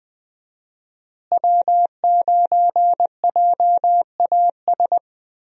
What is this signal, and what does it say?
Morse code, heard as a keyed tone, about 20 wpm, 700 Hz. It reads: W9JAS
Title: USA Call Signs - 5 Characters - Repeated 3x - 20wpm